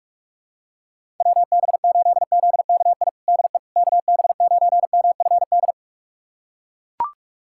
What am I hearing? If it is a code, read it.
Morse code, heard as a keyed tone, about 41 wpm, 690 Hz. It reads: WB8ZKI DE KB9MFD
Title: USA Callsign DE Callsign - 6 Characters - 45wpm